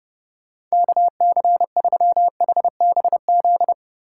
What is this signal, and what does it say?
Morse code, heard as a keyed tone, about 37 wpm, 700 Hz. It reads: KC3HBZ
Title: Speed Racing - USA Call Signs 30wpm